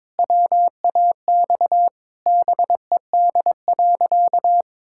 Morse code, heard as a keyed tone, about 22 wpm, 700 Hz. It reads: WAX BED.
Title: Pangram Challenge v2 - 22wpm